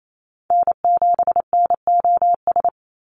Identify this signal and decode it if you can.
Morse code, heard as a keyed tone, about 28 wpm, 700 Hz. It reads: N7NOS
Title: USA Call Signs 28wpm